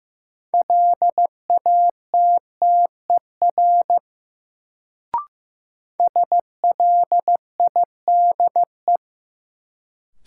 Morse code, heard as a keyed tone, about 15 wpm, 700 Hz. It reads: LATTER SLIDE
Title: Sentences from Top 1600 Words - New Word Drill - 15wpm